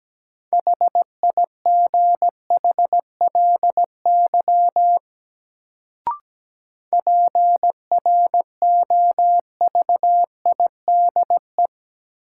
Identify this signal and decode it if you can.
Morse code, heard as a keyed tone, about 17 wpm, 700 Hz. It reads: HIGHLY PROVIDE
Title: Sentences from Top 1100 Words - New Word Drill Alt - 17wpm